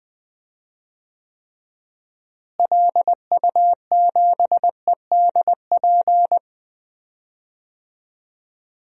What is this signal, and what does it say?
Morse code, heard as a keyed tone, about 20 wpm, 700 Hz. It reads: LU7EDP
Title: DXCC-100 Call Signs - 6 Characters - Repeated 2x - 20wpm